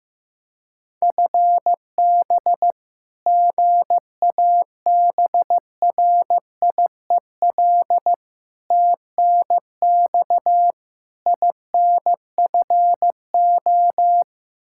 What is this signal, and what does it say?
Morse code, heard as a keyed tone, about 15 wpm, 700 Hz. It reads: FB GABRIEL TNX INFO
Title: Practice QSO - Exchange 3 - More Info - 15wpm